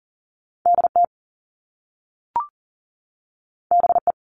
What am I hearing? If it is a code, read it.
Morse code, heard as a keyed tone, about 40 wpm, 700 Hz. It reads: DT BE
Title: All Cut Numbers - 2-Digit 40wpm